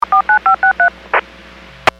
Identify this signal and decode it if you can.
DTMF, heard as telephone touch tones, tones of about 110 ms. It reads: 49563
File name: WPS / 00119 06-00 WPS 451.25 WPS